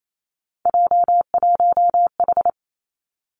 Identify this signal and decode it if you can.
Morse code, heard as a keyed tone, about 28 wpm, 700 Hz. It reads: J1H
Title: Three Challenging Letter-Number Combinations 28wpm